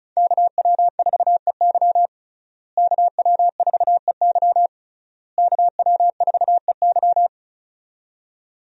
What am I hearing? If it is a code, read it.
Morse code, heard as a keyed tone, about 34 wpm, 700 Hz. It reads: KW4EY KW4EY KW4EY
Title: USA Call Signs - 5 Characters - Repeated 3x - 35wpm